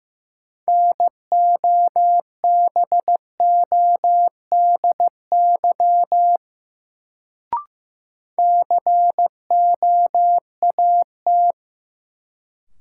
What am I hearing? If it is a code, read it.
Morse code, heard as a keyed tone, about 15 wpm, 700 Hz. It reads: NOBODY COAT
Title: Sentences from Top 1600 Words - New Word Drill - 15wpm